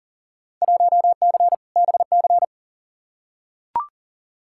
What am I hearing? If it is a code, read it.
Morse code, heard as a keyed tone, about 40 wpm, 700 Hz. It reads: 1C BC
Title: Field Day Exchanges Encore - 40wpm